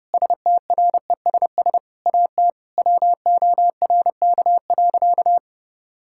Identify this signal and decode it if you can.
Morse code, heard as a keyed tone, about 32 wpm, 700 Hz. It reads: STRESS AT WORK.
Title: Sentences from Top 1000 Words - Review Part 6 - 30wpm